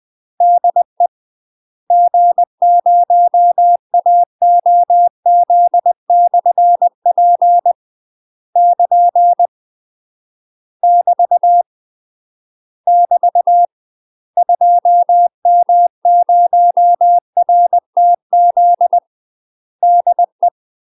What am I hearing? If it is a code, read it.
Morse code, heard as a keyed tone, about 20 wpm, 700 Hz. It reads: DE G0AOZ/P <KN> <BT> <BT> 2M0RTZ DE
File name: UK MASTER QSOs TRAINING / 20WPM / UK MASTER QSOs 0063